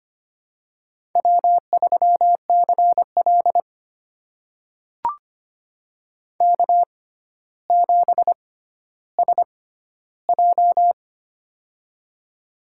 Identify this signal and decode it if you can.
Morse code, heard as a keyed tone, about 25 wpm, spaced as for 13 wpm, 700 Hz. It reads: W3CL K7SJ
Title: USA Call Signs - 4 Characters - 6x Character Spacing - 25wpm